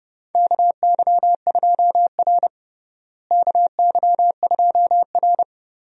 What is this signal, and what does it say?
Morse code, heard as a keyed tone, about 30 wpm, 700 Hz. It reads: KY2R KY2R
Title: USA Call Signs - 4 Characters - Repeated 3x - 30wpm